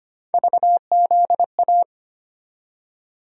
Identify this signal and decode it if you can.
Morse code, heard as a keyed tone, about 25 wpm, 700 Hz. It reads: VZA
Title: USA Call Sign Postfixes - 3 Letters 25wpm